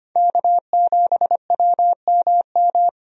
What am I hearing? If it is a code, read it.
Morse code, heard as a keyed tone, about 25 wpm, 700 Hz. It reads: K7WMM
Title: USA Call Signs - 5 Characters - Repeated 3x - 25wpm